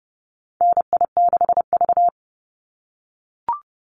Morse code, heard as a keyed tone, about 30 wpm, 700 Hz. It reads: NI6V
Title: Call Signs - 30wpm